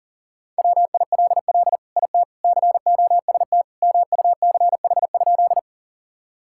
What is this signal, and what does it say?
Morse code, heard as a keyed tone, about 40 wpm, 700 Hz. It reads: WILL IT COST MUCH?
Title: Sentences from Top 1000 Words - Review Part 5 - 40wpm